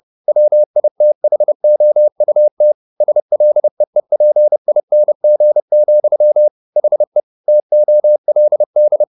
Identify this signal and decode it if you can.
Morse code, heard as a keyed tone, about 30 wpm, 600 Hz. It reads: WITHOUT SLEEPING, HE TOLD